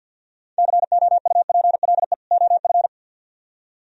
Morse code, heard as a keyed tone, about 63 wpm, 700 Hz. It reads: COUPLE OF